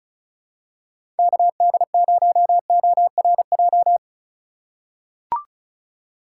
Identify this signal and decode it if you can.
Morse code, heard as a keyed tone, about 36 wpm, 700 Hz. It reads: KD0ORJ
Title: USA Call Signs - 6 Characters - 4x Character Spacing - 35wpm